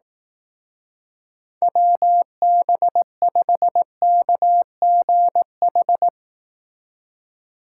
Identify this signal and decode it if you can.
Morse code, heard as a keyed tone, about 18 wpm, 700 Hz. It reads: WB5KGH
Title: Speed Racing - USA 2x3 Call Signs 15wpm